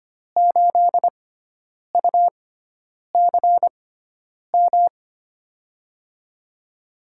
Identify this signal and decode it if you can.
Morse code, heard as a keyed tone, about 25 wpm, spaced as for 9 wpm, 700 Hz. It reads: 8UCM
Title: USA Call Signs - 6x Character Spacing - 25wpm